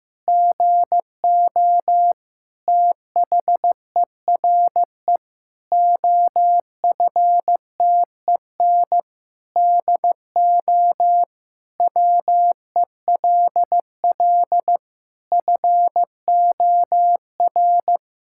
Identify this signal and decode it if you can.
Morse code, heard as a keyed tone, about 15 wpm, 700 Hz. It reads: GO THERE OFTEN DO WELL FOR
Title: Sentences from Top 1000 Words - Review Part 5 - 15wpm